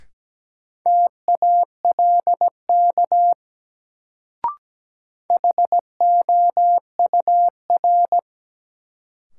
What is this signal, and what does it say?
Morse code, heard as a keyed tone, about 17 wpm, 700 Hz. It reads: TALK HOUR
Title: Sentences from Top 400 Words - New Word Drill - 17wpm